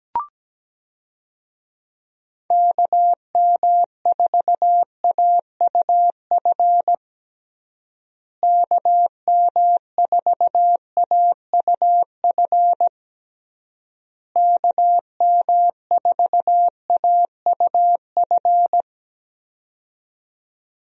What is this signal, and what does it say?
Morse code, heard as a keyed tone, about 17 wpm, 700 Hz. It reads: KM4AUF KM4AUF KM4AUF
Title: USA Call Signs - 6 Characters - Repeated 3x - 17wpm